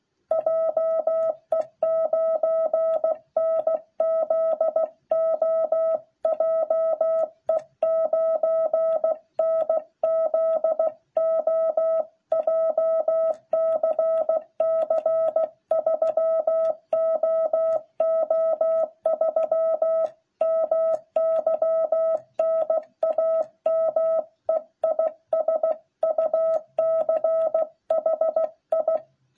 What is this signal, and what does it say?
Morse code, heard as a keyed tone, about 16 wpm, 660 Hz. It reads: JE9NZO JE9NZO JCC 3OO3 MY NAME IS UCHI